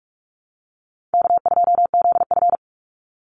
Morse code, heard as a keyed tone, about 46 wpm, 700 Hz. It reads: K2ZF
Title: Easy Call Signs - 45wpm